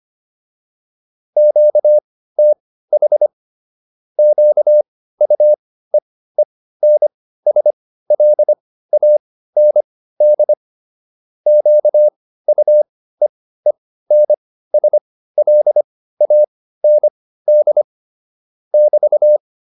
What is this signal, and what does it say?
Morse code, heard as a keyed tone, about 25 wpm, spaced as for 15 wpm, 600 Hz. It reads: QTH QUEENSLAND QUEENSLAND <BT>